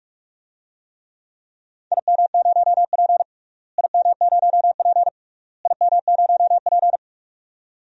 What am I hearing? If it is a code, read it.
Morse code, heard as a keyed tone, about 45 wpm, 690 Hz. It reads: IM0P IM0P IM0P
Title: DXCC-100 Call Signs - 4 Characters - Repeated 3x - 45wpm